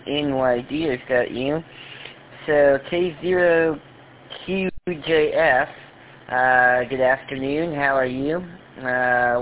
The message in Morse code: none

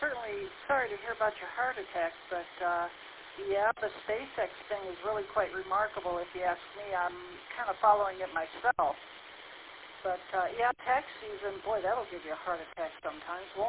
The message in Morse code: none

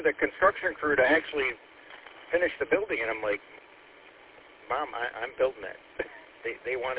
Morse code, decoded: none

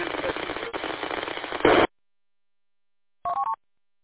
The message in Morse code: none

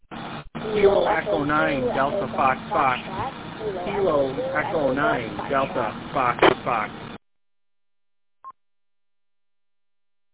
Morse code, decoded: none

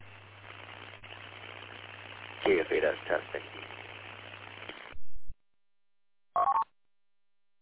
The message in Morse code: none

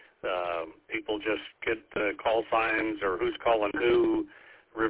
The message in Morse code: none

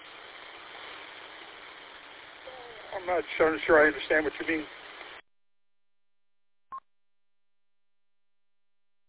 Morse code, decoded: none

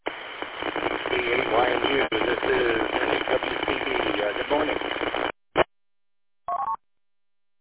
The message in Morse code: none

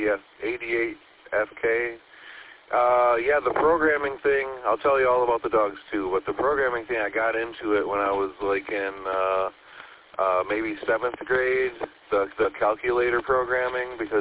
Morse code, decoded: none

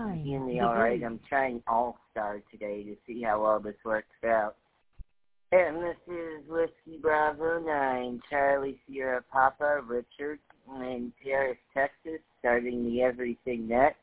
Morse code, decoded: none

